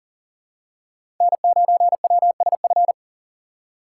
Morse code, heard as a keyed tone, about 40 wpm, 700 Hz. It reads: N9WSF
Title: USA Call Signs - 5 Characters - 4x Character Spacing - 40wpm